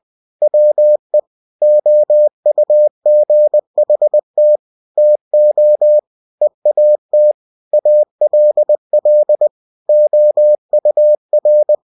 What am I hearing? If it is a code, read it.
Morse code, heard as a keyed tone, about 20 wpm, 600 Hz. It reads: WE OUGHT TO EAT ALL OUR
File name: cw-books / winnie-the-pooh / winnie-20wpm-page-0083